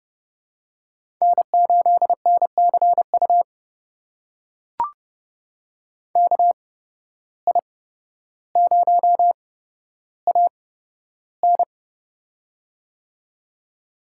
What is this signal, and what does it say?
Morse code, heard as a keyed tone, about 30 wpm, spaced as for 13 wpm, 700 Hz. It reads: N8NCU KI0AN
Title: USA Call Signs - 5 Characters - 8x Character Spacing - 30wpm